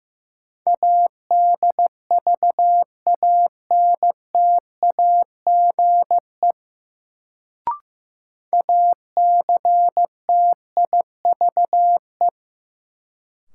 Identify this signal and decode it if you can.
Morse code, heard as a keyed tone, about 15 wpm, 700 Hz. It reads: ADVANTAGE ACTIVE
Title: Sentences from Top 900 Words - New Word Drill - 15wpm